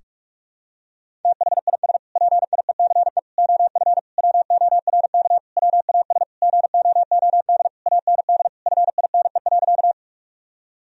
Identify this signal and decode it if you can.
Morse code, heard as a keyed tone, about 41 wpm, 700 Hz. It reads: THIS PIECE OF WORK WAS GOOD AND FINE.
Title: Sentences from Top 1000 Words - Review Part 2 - 45wpm